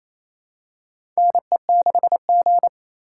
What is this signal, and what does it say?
Morse code, heard as a keyed tone, about 28 wpm, 700 Hz. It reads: NE6G
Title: USA Call Signs - 4 Characters - 3x Character Spacing - 28wpm